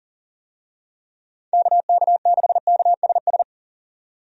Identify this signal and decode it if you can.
Morse code, heard as a keyed tone, about 40 wpm, 700 Hz. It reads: KK6KSS
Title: USA Call Signs - 6 Characters - 40wpm